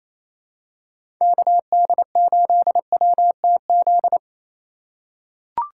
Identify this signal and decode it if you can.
Morse code, heard as a keyed tone, about 28 wpm, 700 Hz. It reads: KD8WTZ
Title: USA Call Signs - 6 Characters - 4x Character Spacing - 28wpm